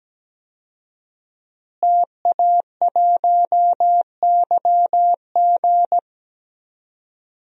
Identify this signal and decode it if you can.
Morse code, heard as a keyed tone, about 17 wpm, 700 Hz. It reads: TA1YG
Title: DXCC-100 Call Signs - 5 Characters - Repeated 3x - 17wpm